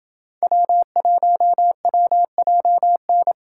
Morse code, heard as a keyed tone, about 27 wpm, 700 Hz. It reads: W1WJN